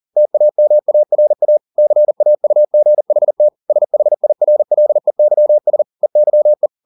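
Morse code, heard as a keyed tone, about 40 wpm, 600 Hz. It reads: TAMARA CAUGHT SHIRLEYS EYE
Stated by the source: Morse code at 40 wpm, practice text, CW med Gnister.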